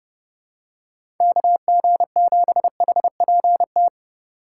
Morse code, heard as a keyed tone, about 30 wpm, 700 Hz. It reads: KG7HPT
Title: USA Call Signs - 4x Character Spacing - 30wpm